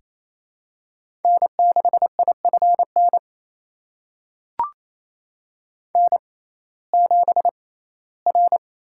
Morse code, heard as a keyed tone, about 28 wpm, spaced as for 17 wpm, 700 Hz. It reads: N6IFN N7R
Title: USA Call Signs - 5 Characters - 6x Character Spacing - 28wpm